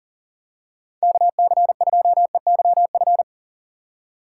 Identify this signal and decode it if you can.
Morse code, heard as a keyed tone, about 39 wpm, 700 Hz. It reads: KC2EYF
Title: USA Call Signs - 6 Characters - 3x Character Spacing - 40wpm